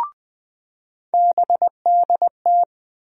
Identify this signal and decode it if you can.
Morse code, heard as a keyed tone, about 20 wpm, 700 Hz. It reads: BDT
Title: USA Call Sign Postfixes - 3 Letters 20wpm